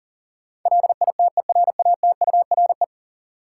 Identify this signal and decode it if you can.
Morse code, heard as a keyed tone, about 40 wpm, 700 Hz. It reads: LITERATURE
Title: Sentences from Top 600 Words - New Word Drill - 40wpm